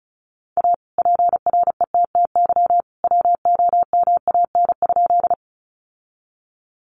Morse code, heard as a keyed tone, about 34 wpm, 700 Hz. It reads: A PRETTY WOMAN?
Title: Sentences from Top 1100 Words 35wpm